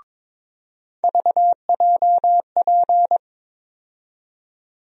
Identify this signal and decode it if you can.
Morse code, heard as a keyed tone, about 21 wpm, 700 Hz. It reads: VJP
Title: USA Call Sign Postfixes - 3 Letters 22wpm